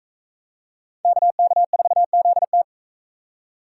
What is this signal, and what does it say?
Morse code, heard as a keyed tone, about 43 wpm, 700 Hz. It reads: KK4ZT